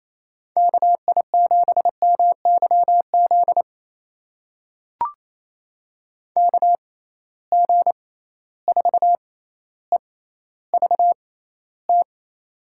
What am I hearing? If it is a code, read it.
Morse code, heard as a keyed tone, about 28 wpm, spaced as for 14 wpm, 700 Hz. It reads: KI7MYZ KG4EVT
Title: USA Call Signs - 6 Characters - 6x Character Spacing - 28wpm